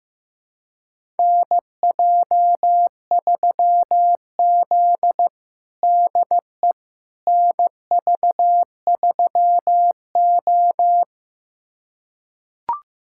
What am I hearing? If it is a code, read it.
Morse code, heard as a keyed tone, about 16 wpm, 700 Hz. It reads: NJ3Z DE NV3O
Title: USA Callsign DE Callsign - 4 Characters - 15wpm